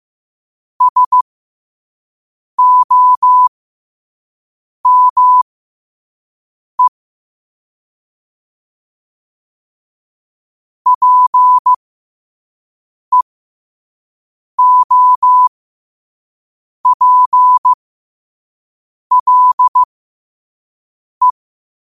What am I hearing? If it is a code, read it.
Morse code, heard as a keyed tone, about 15 wpm, spaced as for 5 wpm, 1000 Hz. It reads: SOME PEOPLE
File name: qotdfiles / 5wpm